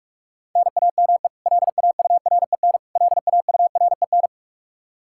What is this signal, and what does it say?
Morse code, heard as a keyed tone, about 46 wpm, 700 Hz. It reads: NAME LAUREN LAUREN
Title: Practice QSO - Exchange 1 - Beginning - 45wpm